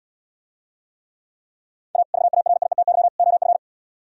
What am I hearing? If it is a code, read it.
Morse code, heard as a keyed tone, about 77 wpm, 700 Hz. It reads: A VARIETY OF